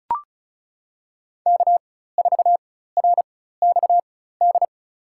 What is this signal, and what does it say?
Morse code, heard as a keyed tone, about 34 wpm, spaced as for 16 wpm, 700 Hz. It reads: K4RXD
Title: USA Call Signs - 5 Characters - 4x Character Spacing - 35wpm